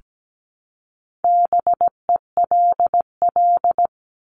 Morse code, heard as a keyed tone, about 17 wpm, 700 Hz. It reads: BELL
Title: Top 1751-2000 Words 17wpm